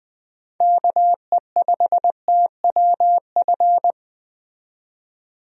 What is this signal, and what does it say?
Morse code, heard as a keyed tone, about 20 wpm, 700 Hz. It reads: KE5TWF
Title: USA Call Signs - 6 Characters - 20wpm